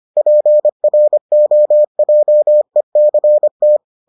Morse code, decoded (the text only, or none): PROJECT